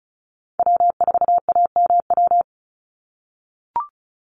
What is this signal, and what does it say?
Morse code, heard as a keyed tone, about 34 wpm, 700 Hz. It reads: W4AMW
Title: NA SOTA Chaser Callsigns 35wpm